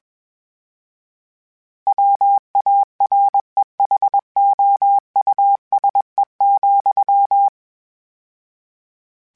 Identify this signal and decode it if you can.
Morse code, heard as a keyed tone, about 21 wpm, 800 Hz. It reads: WAREHOUSE,
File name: General / General-mp3s / Houses @20